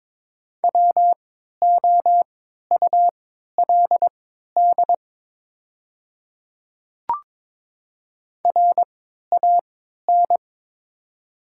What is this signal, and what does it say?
Morse code, heard as a keyed tone, about 22 wpm, spaced as for 13 wpm, 700 Hz. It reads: WOULD RAN
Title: 3x Character Spacing - Top 500 Words - 22wpm